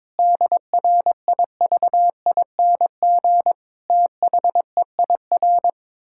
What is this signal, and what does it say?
Morse code, heard as a keyed tone, about 22 wpm, 700 Hz. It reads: DRIVING THEIR
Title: Pangram Challenge v2 - 22wpm